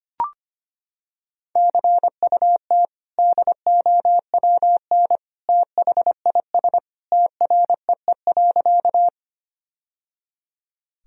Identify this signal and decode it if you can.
Morse code, heard as a keyed tone, about 25 wpm, 700 Hz. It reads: CUT DOWN THIS TREE.
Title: Sentences from Top 1000 Words - Review Part 1 - 25wpm